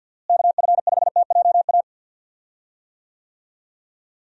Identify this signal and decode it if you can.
Morse code, heard as a keyed tone, about 51 wpm, 700 Hz. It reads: KF5TJA
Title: USA Call Signs - 6 Characters - Repeated 2x - 50wpm